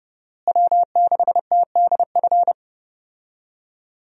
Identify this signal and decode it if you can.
Morse code, heard as a keyed tone, about 30 wpm, 700 Hz. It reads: W6TDF
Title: USA Call Signs - 5 Characters - Repeated 3x - 30wpm